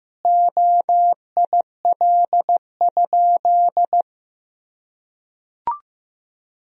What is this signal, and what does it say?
Morse code, heard as a keyed tone, about 15 wpm, 700 Hz. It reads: OIL?
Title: Sentences from Top 1000 Words - Review Part 4 - 15wpm